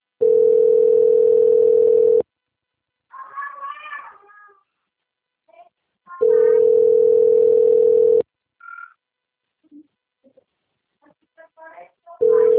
Ringback tone (North America), 440 + 480 Hz.